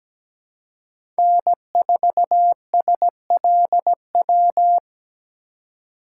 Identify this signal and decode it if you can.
Morse code, heard as a keyed tone, about 17 wpm, 700 Hz. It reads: N4SLW
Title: USA Call Signs - 2x Character Spacing - 17wpm